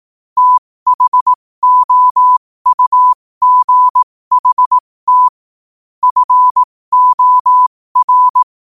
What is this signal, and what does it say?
Morse code, heard as a keyed tone, about 18 wpm, spaced as for 15 wpm, 1000 Hz. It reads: THOUGHT FOR
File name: qotdfiles / 15wpm